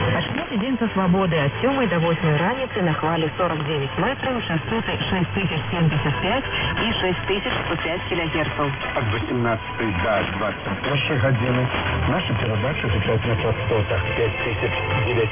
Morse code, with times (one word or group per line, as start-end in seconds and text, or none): none